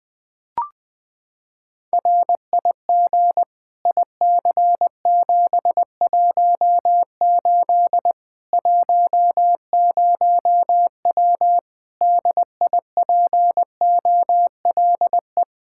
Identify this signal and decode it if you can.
Morse code, heard as a keyed tone, about 20 wpm, 700 Hz. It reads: RIG IC718 10W DIPOLE
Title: Practice QSO - Exchange 2 - Info - 20wpm